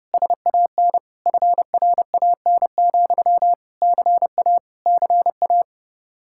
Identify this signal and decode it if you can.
Morse code, heard as a keyed tone, about 30 wpm, 700 Hz. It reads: SAN FRAN, CA CA